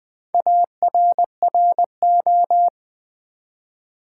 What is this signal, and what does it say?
Morse code, heard as a keyed tone, about 20 wpm, 700 Hz. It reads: ARRO
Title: Build A Word - 1 Letter At A Time - 3 to 5-Letter Words - 20wpm